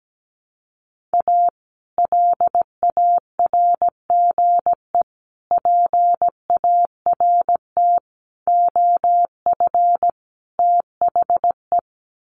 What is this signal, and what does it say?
Morse code, heard as a keyed tone, about 17 wpm, 700 Hz. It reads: A LARGE PART OF THE
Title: Sets of 5 Words Encore2 17wpm